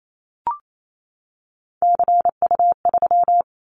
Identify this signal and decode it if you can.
Morse code, heard as a keyed tone, about 28 wpm, 700 Hz. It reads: CU3
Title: Callsign Prefixes for Top 100 DXCC - 28wpm